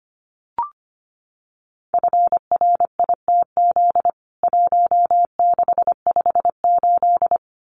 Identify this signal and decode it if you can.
Morse code, heard as a keyed tone, about 25 wpm, 700 Hz. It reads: FRITZ 1658
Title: CWT Exchange 25wpm